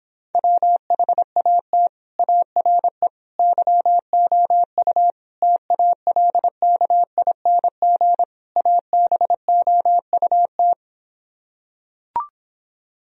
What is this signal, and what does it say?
Morse code, heard as a keyed tone, about 26 wpm, 700 Hz. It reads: WHAT ARE YOU TALKING ABOUT